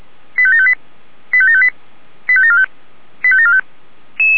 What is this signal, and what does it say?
Five-tone selective call (ZVEI), heard as quick runs of five tones, tones of about 70 ms. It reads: 85758 85758 85745 85745
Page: /464/SDR/ZVEI/